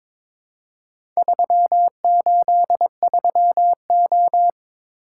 Morse code, heard as a keyed tone, about 23 wpm, 690 Hz. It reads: 383O
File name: practice / 22wpm / Four Letter-Number Combinations - 22wpm